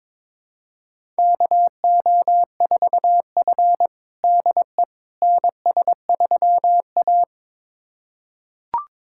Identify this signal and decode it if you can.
Morse code, heard as a keyed tone, about 22 wpm, 700 Hz. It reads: KO4F DE NS3A
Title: USA Callsign DE Callsign - 4 Characters - 22wpm